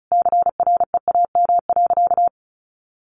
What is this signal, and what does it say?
Morse code, heard as a keyed tone, about 34 wpm, 700 Hz. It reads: CREAM.
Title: Sentences from Top 1500 Words 35wpm